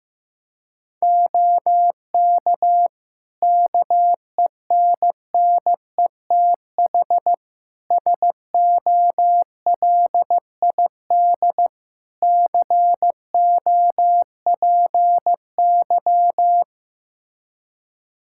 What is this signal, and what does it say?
Morse code, heard as a keyed tone, about 15 wpm, 700 Hz. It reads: OK KENNETH SOLID COPY
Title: Practice QSO - Exchange 3 - More Info - 15wpm